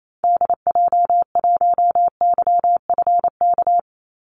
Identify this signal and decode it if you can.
Morse code, heard as a keyed tone, about 28 wpm, 700 Hz. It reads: DJ1YFK
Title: Call Signs Repeated 2x - 28wpm